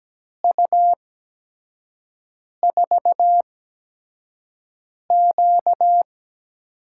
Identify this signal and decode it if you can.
Morse code, heard as a keyed tone, about 17 wpm, spaced as for 5 wpm, 700 Hz. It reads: U4Q
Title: USA Call Signs - 4 Characters - 8x Character Spacing - 17wpm